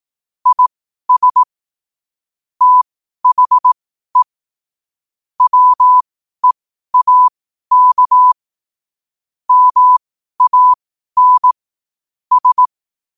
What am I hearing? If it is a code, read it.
Morse code, heard as a keyed tone, about 18 wpm, spaced as for 12 wpm, 1000 Hz. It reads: IS THE WEAK MAN S
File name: qotdfiles / 12wpm